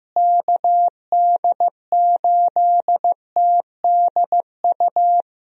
Morse code, heard as a keyed tone, about 15 wpm, 700 Hz. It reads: KD8TDU
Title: USA Call Signs - 6 Characters - Repeated 3x - 15wpm